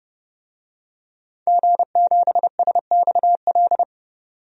Morse code, heard as a keyed tone, about 29 wpm, 700 Hz. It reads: G7SXL
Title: DXCC-100 Call Signs - 5 Characters - 30wpm